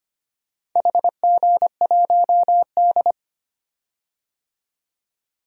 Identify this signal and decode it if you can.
Morse code, heard as a keyed tone, about 25 wpm, 700 Hz. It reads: HG1D
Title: DXCC-100 Call Signs - 4 Characters - 25wpm